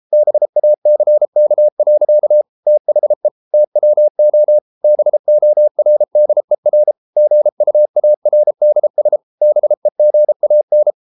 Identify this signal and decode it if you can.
Morse code, heard as a keyed tone, about 33 wpm, 600 Hz. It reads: BACK. THE TWO BORDER GUARDS BEGAN